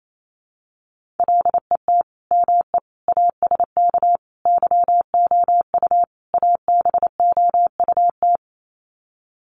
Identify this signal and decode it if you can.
Morse code, heard as a keyed tone, about 28 wpm, 700 Hz. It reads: LET ME ASK YOU ABOUT
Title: Sets of 5 Words 28wpm